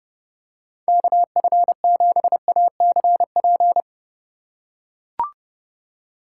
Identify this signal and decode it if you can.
Morse code, heard as a keyed tone, about 30 wpm, 700 Hz. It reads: KF7ACP